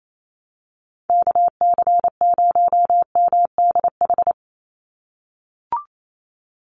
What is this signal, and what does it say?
Morse code, heard as a keyed tone, about 28 wpm, 700 Hz. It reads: KC0MDH
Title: USA Call Signs 28wpm